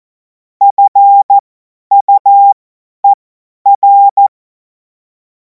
Morse code, heard as a keyed tone, about 14 wpm, spaced as for 10 wpm, 800 Hz. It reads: FUER
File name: Morsekurs / Witze / Witz 17. Februar 2026 50 F70